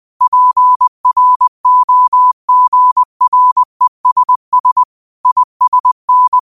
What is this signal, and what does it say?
Morse code, heard as a keyed tone, about 20 wpm, 1000 Hz. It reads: PROGRESS ISN
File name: qotdfiles / 20wpm